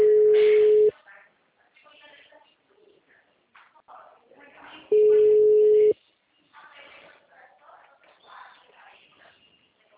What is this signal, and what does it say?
Ringback tone (Europe, 425 Hz), with speech.